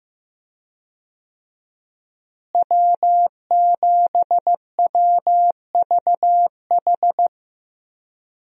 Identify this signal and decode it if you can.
Morse code, heard as a keyed tone, about 15 wpm, 700 Hz. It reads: W7WVH
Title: USA Call Signs - 5 Characters - Repeated 2x - 15wpm